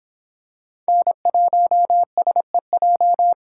Morse code, heard as a keyed tone, about 26 wpm, 700 Hz. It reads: N1SEJ